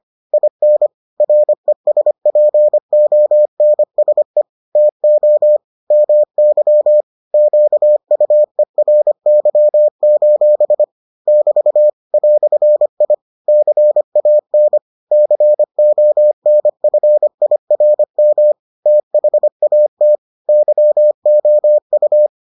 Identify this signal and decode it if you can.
Morse code, heard as a keyed tone, about 25 wpm, 600 Hz. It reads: IN RESPONSE TO MY QUERY: <BT> "I CAN CONFIRM THAT YOU